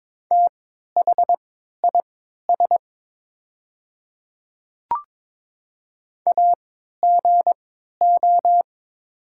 Morse code, heard as a keyed tone, about 22 wpm, spaced as for 12 wpm, 700 Hz. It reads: THIS AGO
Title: 3x Character Spacing - Top 500 Words - 22wpm